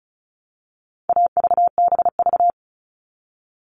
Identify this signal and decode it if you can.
Morse code, heard as a keyed tone, about 34 wpm, 700 Hz. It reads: AVBV
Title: All Cut Numbers - 4-Digit 35wpm